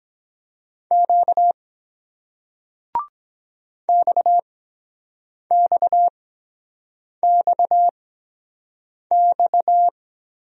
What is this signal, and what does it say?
Morse code, heard as a keyed tone, about 23 wpm, spaced as for 7 wpm, 700 Hz. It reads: Q XXXX